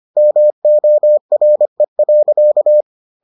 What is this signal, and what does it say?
Morse code, heard as a keyed tone, about 25 wpm, 600 Hz. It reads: MORE.